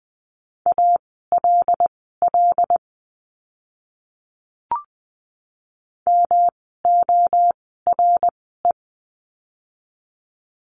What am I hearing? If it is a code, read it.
Morse code, heard as a keyed tone, about 20 wpm, spaced as for 15 wpm, 700 Hz. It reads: ALL MORE
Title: 244 Lesson 39f - Letter z - Words 20wpm